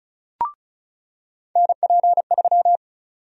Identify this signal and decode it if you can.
Morse code, heard as a keyed tone, about 36 wpm, 700 Hz. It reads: NP3
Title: USA Call Sign Prefixes - 35wpm